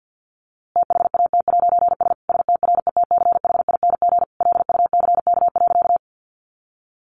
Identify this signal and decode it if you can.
Morse code, heard as a keyed tone, about 50 wpm, 700 Hz. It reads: THAT'S STRETCHING LUCK.
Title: A Sound Of Thunder 50wpm